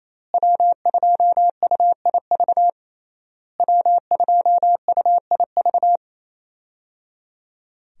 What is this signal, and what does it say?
Morse code, heard as a keyed tone, about 27 wpm, 700 Hz. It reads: W2UIV W2UIV
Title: USA Call Signs - 5 Characters - Repeated 3x - 28wpm